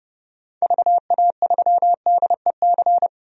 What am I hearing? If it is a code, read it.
Morse code, heard as a keyed tone, about 30 wpm, 700 Hz. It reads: VA3DEC
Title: DXCC-100 Call Signs - 6 Characters - 30wpm